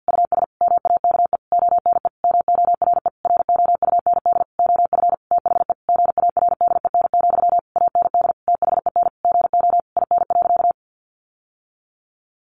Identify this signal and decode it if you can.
Morse code, heard as a keyed tone, about 48 wpm, 700 Hz. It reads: US MAKE ONE MORE ROUND OF THE GARDEN, AND THEN GO IN.
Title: Sentences from Top 700 Words 50wpm